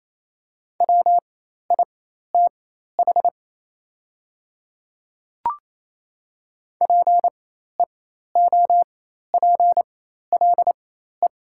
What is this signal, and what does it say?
Morse code, heard as a keyed tone, about 28 wpm, spaced as for 13 wpm, 700 Hz. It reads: WITH PEOPLE
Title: 4x Character Spacing - Top 100 Words - 28wpm